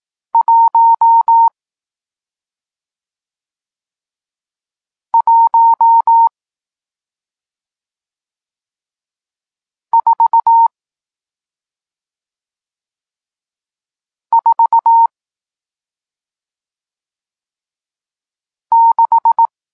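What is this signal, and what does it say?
Morse code, heard as a keyed tone, about 18 wpm, spaced as for 2 wpm, 920 Hz. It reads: 11446
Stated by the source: Morse code lesson teaching digits 1, 4 and 6, from SP9MOA, Niepołomicki Klub Krótkofalowców (Poland) amateur radio club.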